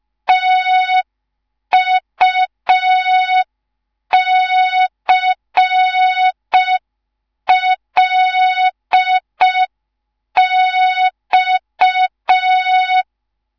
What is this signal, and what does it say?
Morse code, heard as a keyed tone, about 5 wpm, 3700 Hz. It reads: TUCLX